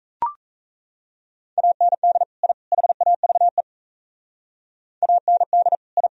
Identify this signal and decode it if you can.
Morse code, heard as a keyed tone, about 37 wpm, 700 Hz. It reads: AND I HAVE AND I